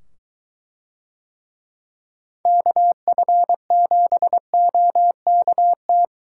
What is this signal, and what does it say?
Morse code, heard as a keyed tone, about 21 wpm, 700 Hz. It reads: KF7OKT